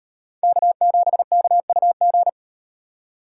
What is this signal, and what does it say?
Morse code, heard as a keyed tone, about 38 wpm, 700 Hz. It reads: K7KUG